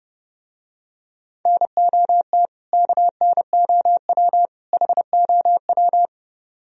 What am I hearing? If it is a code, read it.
Morse code, heard as a keyed tone, about 29 wpm, 700 Hz. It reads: NOT KNOW HOW